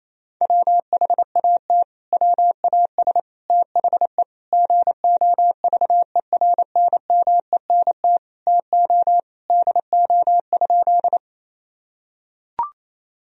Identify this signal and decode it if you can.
Morse code, heard as a keyed tone, about 28 wpm, 700 Hz. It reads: WHAT WAS THE GOVERNMENT TO DO?
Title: Sentences from Top 1000 Words - Review Part 3 - 28wpm